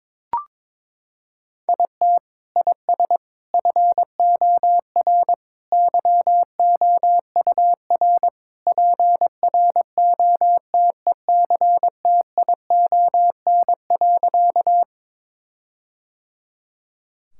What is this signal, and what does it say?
Morse code, heard as a keyed tone, about 22 wpm, 700 Hz. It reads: IT IS FOR YOUR PROTECTION.
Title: Sentences from Top 1600 Words 22wpm